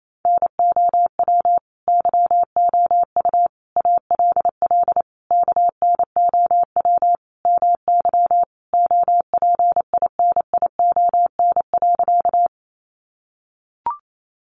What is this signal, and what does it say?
Morse code, heard as a keyed tone, about 28 wpm, 700 Hz. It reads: NOW YOU ALL KNOW MY OPINION.
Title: Sentences from Top 1200 Words 28wpm